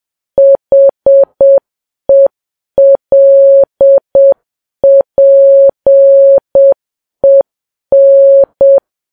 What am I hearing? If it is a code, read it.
Morse code, heard as a keyed tone, about 7 wpm, 550 Hz. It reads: HELPEN
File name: Files Web tls / Morse on9fv / morse7kps